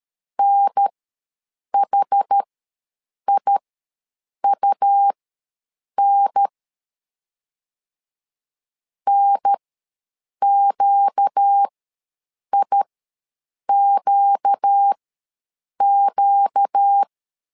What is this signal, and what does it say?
Morse code, heard as a keyed tone, about 13 wpm, spaced as for 7 wpm, 800 Hz. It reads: NHIUN NQIQQ
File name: LEZIONI / 4 Q ws 12wpm 1912